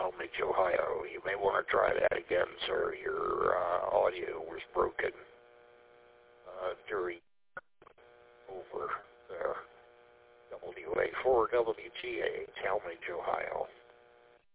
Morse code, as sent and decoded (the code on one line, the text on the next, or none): none